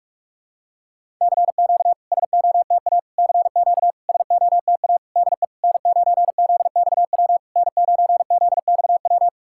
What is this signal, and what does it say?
Morse code, heard as a keyed tone, about 46 wpm, 690 Hz. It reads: CQ SOTA CQ SOTA DE N9ZXW N9ZXW
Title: SOTA QSO Practice 45wpm